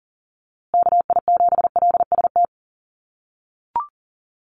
Morse code, heard as a keyed tone, about 40 wpm, 700 Hz. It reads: KI7LST
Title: USA Call Signs Repeated 2x 40wpm